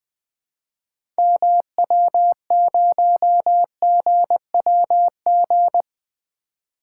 Morse code, heard as a keyed tone, about 20 wpm, 700 Hz. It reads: MW0GWG
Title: 2023 POTA Activator Call Signs - 20wpm